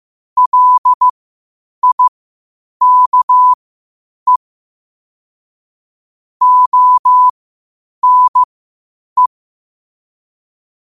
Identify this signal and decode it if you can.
Morse code, heard as a keyed tone, about 15 wpm, spaced as for 8 wpm, 1000 Hz. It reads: LIKE ONE
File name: 8wpm